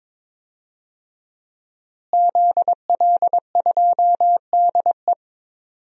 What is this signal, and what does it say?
Morse code, heard as a keyed tone, about 21 wpm, 700 Hz. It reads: ZL2DE
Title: DXCC-100 Call Signs - 5 Characters - Repeated 3x - 22wpm